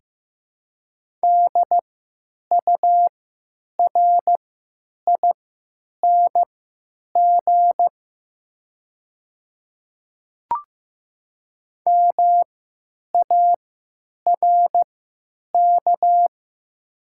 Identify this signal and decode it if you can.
Morse code, heard as a keyed tone, about 15 wpm, spaced as for 9 wpm, 700 Hz. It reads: DURING MARK
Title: 3x Character Spacing - Top 500 Words - 15wpm